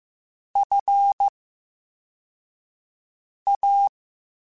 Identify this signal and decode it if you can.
Morse code, heard as a keyed tone, about 15 wpm, spaced as for 4 wpm, 800 Hz. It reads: FA
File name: Special INGs / Special INGs-mp3s / Slow ing @13